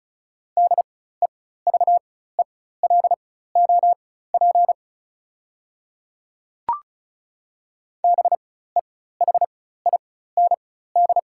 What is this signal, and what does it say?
Morse code, heard as a keyed tone, about 36 wpm, spaced as for 16 wpm, 700 Hz. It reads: DEVELOP BEHIND
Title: 4x Character Spacing - Top 500 Words - 35wpm